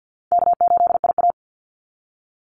Morse code, heard as a keyed tone, about 50 wpm, 700 Hz. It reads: K8IA
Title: Easy Call Signs Repeated 3x - 50wpm